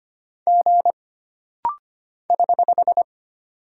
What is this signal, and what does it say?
Morse code, heard as a keyed tone, about 25 wpm, 700 Hz. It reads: G <HH>
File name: Single Character (Letter:Number:Punctuation:Prosign) - Rapid-Fire - 25wpm